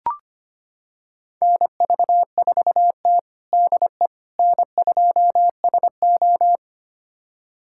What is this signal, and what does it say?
Morse code, heard as a keyed tone, about 25 wpm, 700 Hz. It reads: NV4T DE N2SO
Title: USA Callsign DE Callsign - 4 Characters - 25wpm